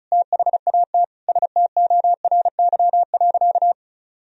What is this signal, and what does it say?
Morse code, heard as a keyed tone, about 34 wpm, 700 Hz. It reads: THAT STORY.
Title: Sentences from Top 1000 Words - Review Part 1 - 35wpm